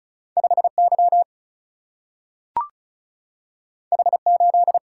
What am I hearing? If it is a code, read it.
Morse code, heard as a keyed tone, about 36 wpm, 700 Hz. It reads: 5Y H8
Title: Single-Letter Single-Number 35wpm